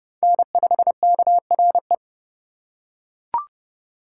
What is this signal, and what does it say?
Morse code, heard as a keyed tone, about 29 wpm, 700 Hz. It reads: N5KRE